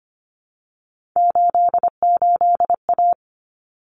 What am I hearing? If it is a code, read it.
Morse code, heard as a keyed tone, about 25 wpm, 700 Hz. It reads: 88A
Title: Common Cut Numbers - 3-Digit 25wpm